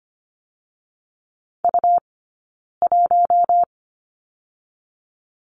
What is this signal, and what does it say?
Morse code, heard as a keyed tone, about 25 wpm, 700 Hz. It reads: U 1
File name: course / 25:10wpm / 099 Lesson 17d - Letter u - Two Characters 25wpm